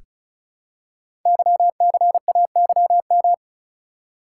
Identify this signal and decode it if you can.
Morse code, heard as a keyed tone, about 36 wpm, 700 Hz. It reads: YCAYM
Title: Code Groups - 5 Letters - 35wpm